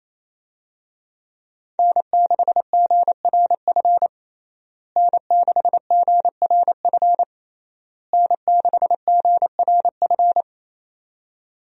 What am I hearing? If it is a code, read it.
Morse code, heard as a keyed tone, about 28 wpm, 700 Hz. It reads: N6GRF N6GRF N6GRF
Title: USA Call Signs - 5 Characters - Repeated 3x - 28wpm